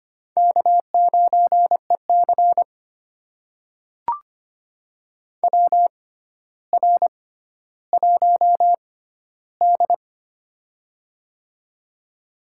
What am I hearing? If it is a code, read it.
Morse code, heard as a keyed tone, about 25 wpm, spaced as for 13 wpm, 700 Hz. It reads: K9EC WR1D
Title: USA Call Signs - 4 Characters - 6x Character Spacing - 25wpm